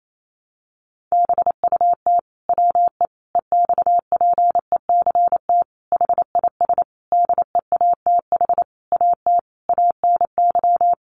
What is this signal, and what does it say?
Morse code, heard as a keyed tone, about 28 wpm, 700 Hz. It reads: BUT WE EXPECT HIS DEATH AT ANY
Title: Sentences from Top 1100 Words 28wpm